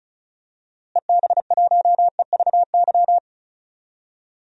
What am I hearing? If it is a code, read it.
Morse code, heard as a keyed tone, about 34 wpm, 700 Hz. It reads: EB1EVY
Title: DXCC-100 Call Signs - 6 Characters - Repeated 3x - 35wpm